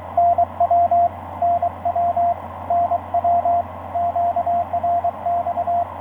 Morse code, heard as a keyed tone, about 23 wpm, 690 Hz, with radio noise. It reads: NW NW NW QRX
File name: zd9t oz8bte etc 12m cw 22nov2010 1407